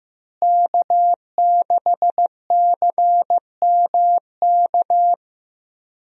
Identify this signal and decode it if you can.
Morse code, heard as a keyed tone, about 15 wpm, 700 Hz. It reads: K6CMK
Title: USA Call Signs - 5 Characters - Repeated 3x - 15wpm